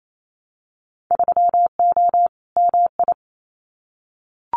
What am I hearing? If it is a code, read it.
Morse code, heard as a keyed tone, about 28 wpm, 700 Hz. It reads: 3O MI
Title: Winter Field Day Exchanges 28wpm